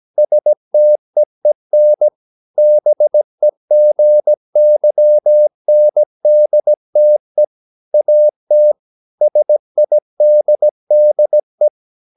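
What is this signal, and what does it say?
Morse code, heard as a keyed tone, about 17 wpm, 600 Hz. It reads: STEEN BEGYNDTE AT SIDDE
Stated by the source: Slow (QRS) Morse practice at 17 wpm, from CW med Gnister.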